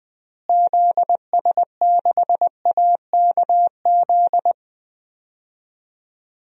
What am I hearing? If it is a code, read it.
Morse code, heard as a keyed tone, about 20 wpm, 700 Hz. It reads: ZS6AKZ
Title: DXCC-100 Call Signs - 6 Characters - Repeated 3x - 20wpm